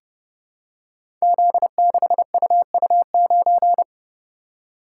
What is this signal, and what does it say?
Morse code, heard as a keyed tone, about 30 wpm, 700 Hz. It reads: Z6UU9
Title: Code Groups - 5 Letter:Numbers - 30wpm